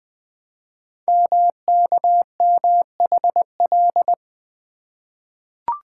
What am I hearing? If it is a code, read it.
Morse code, heard as a keyed tone, about 20 wpm, 700 Hz. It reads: MKMHL
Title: Code Groups - 5 Letters - 20wpm